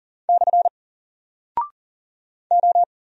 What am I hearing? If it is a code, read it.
Morse code, heard as a keyed tone, about 40 wpm, 700 Hz. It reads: / O
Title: Single Character (Letter:Number:Punctuation:Prosign) Alt 40wpm